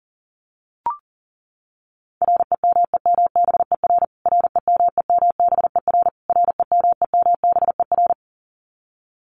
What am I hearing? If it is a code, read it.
Morse code, heard as a keyed tone, about 40 wpm, 700 Hz. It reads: REMEMBER REMEMBER REMEMBER
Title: Top 500 Words Repeated 40wpm